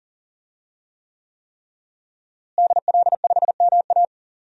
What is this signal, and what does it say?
Morse code, heard as a keyed tone, about 40 wpm, 700 Hz. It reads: DL5MA
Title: DXCC-100 Call Signs - 5 Characters - Repeated 3x - 40wpm